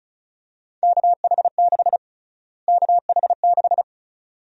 Morse code, heard as a keyed tone, about 36 wpm, 700 Hz. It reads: KH6 KH6
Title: USA Call Sign Prefixes - Repeated 2x - 35wpm